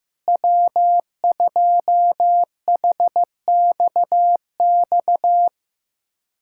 Morse code, heard as a keyed tone, about 15 wpm, 700 Hz. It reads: W2HXX
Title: USA Callsign DE Callsign - 5 Characters - 15wpm